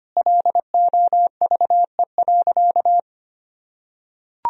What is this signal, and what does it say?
Morse code, heard as a keyed tone, about 25 wpm, 700 Hz. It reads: LOVE.
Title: Sentences from Top 1000 Words - Review Part 2 - 25wpm